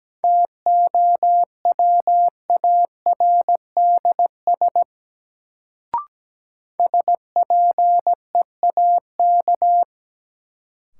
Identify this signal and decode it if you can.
Morse code, heard as a keyed tone, about 17 wpm, 700 Hz. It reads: TOWARDS SPEAK
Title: Sentences from Top 1000 Words - Review Part 2 - New Word Drill - 17wpm